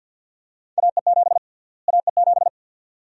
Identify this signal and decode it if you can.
Morse code, heard as a keyed tone, about 51 wpm, 700 Hz. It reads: AE7 AE7
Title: USA Call Sign Prefixes - Repeated 2x - 50wpm